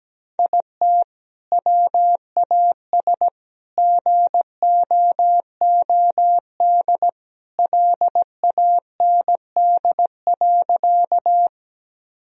Morse code, heard as a keyed tone, about 17 wpm, 700 Hz. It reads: IT WAS GOOD LAND.
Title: Sentences from Top 1000 Words - Review Part 1 - 17wpm